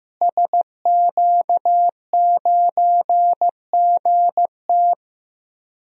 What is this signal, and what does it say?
Morse code, heard as a keyed tone, about 15 wpm, 700 Hz. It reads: SQ9GT
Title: DXCC-100 Call Signs - 5 Characters - Repeated 3x - 15wpm